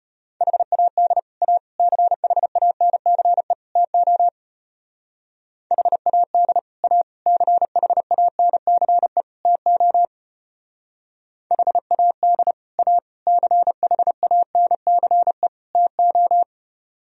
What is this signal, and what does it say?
Morse code, heard as a keyed tone, about 33 wpm, 700 Hz. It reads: HAD A CHANCE TO HAD A CHANCE TO HAD A CHANCE TO